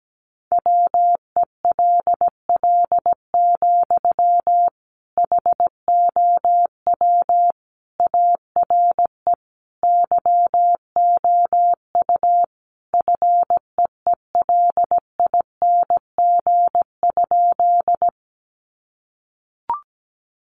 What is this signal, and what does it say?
Morse code, heard as a keyed tone, about 17 wpm, 700 Hz. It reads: WELL, HOW ARE YOU FEELING?
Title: Sentences from Top 1200 Words 17wpm